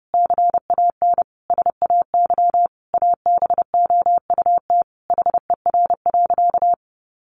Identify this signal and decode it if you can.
Morse code, heard as a keyed tone, about 30 wpm, 700 Hz. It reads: CAN SAY ABOUT HER.
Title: Sentences from Top 200 Words 30wpm